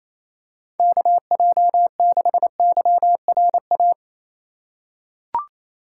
Morse code, heard as a keyed tone, about 27 wpm, 700 Hz. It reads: KJ6YRA
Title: USA Call Signs - 6 Characters - 3x Character Spacing - 28wpm